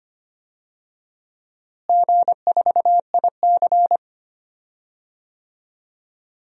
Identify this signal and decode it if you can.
Morse code, heard as a keyed tone, about 25 wpm, 700 Hz. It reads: G4IC
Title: DXCC-100 Call Signs - 4 Characters - Repeated 2x - 25wpm